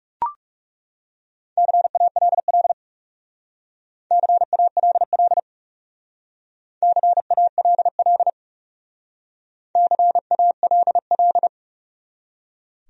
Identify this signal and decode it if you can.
Morse code, heard as a keyed tone, about 36 wpm, 700 Hz. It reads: CALL CALL CALL CALL
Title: Speed-Racing Top 300 Words 30wpm